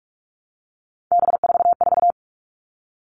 Morse code, heard as a keyed tone, about 44 wpm, 700 Hz. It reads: B44
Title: All Cut Numbers - 3-Digit 45wpm